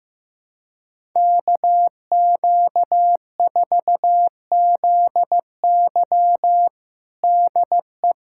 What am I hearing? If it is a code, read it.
Morse code, heard as a keyed tone, about 15 wpm, 700 Hz. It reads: KQ4ZY DE
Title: USA Callsign DE Callsign - 5 Characters - 15wpm